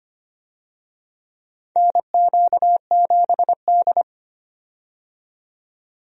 Morse code, heard as a keyed tone, about 25 wpm, 700 Hz. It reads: NQ7D
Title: USA Call Signs - 4 Characters - 25wpm